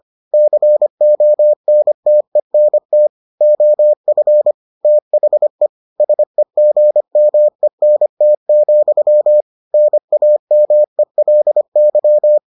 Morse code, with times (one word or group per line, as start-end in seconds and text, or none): 0.3-3.1 CONTENT
3.4-4.5 OF
4.8-5.7 THE
6.0-9.4 SEGMENT,
9.7-12.4 NAMELY